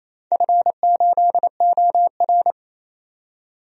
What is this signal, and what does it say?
Morse code, heard as a keyed tone, about 29 wpm, 700 Hz. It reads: F8OR
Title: DXCC-100 Call Signs - 4 Characters - 28wpm